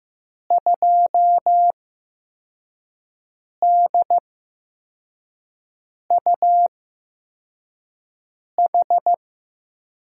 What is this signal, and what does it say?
Morse code, heard as a keyed tone, about 15 wpm, spaced as for 4 wpm, 700 Hz. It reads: 2DUH
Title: USA Call Signs - 5 Characters - 8x Character Spacing - 15wpm